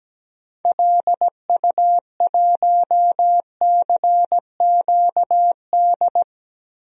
Morse code, heard as a keyed tone, about 17 wpm, 700 Hz. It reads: LU1CQD
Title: DXCC-100 Call Signs - 6 Characters - 17wpm